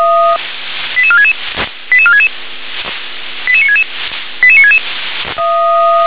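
Five-tone selective call (ZVEI), heard as quick runs of five tones, tones of about 70 ms. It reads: 88466 88466 88966 88966
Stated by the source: ZVEI five-tone selective call received on an SDR.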